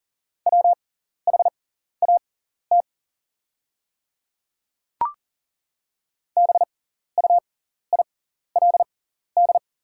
Morse code, heard as a keyed tone, about 40 wpm, spaced as for 14 wpm, 690 Hz. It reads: WHAT BUILD
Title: 6x Character Spacing - Top 500 Words - 40wpm